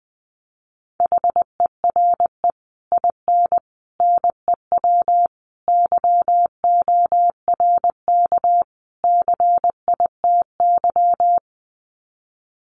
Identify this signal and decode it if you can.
Morse code, heard as a keyed tone, about 19 wpm, 700 Hz. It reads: HERE IN NEW YORK CITY
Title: Sets of 5 Words Encore2 20wpm